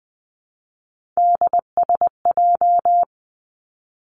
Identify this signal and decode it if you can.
Morse code, heard as a keyed tone, about 20 wpm, 700 Hz. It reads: DSJ
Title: Three Challenging Letter-Number Combinations 20wpm